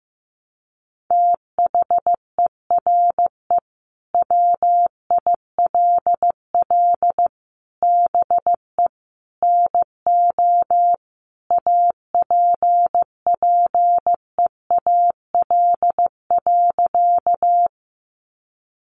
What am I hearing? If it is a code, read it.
Morse code, heard as a keyed tone, about 15 wpm, 700 Hz. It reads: THERE WILL BE NO APPEAL.
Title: Sentences from Top 1300 Words 15wpm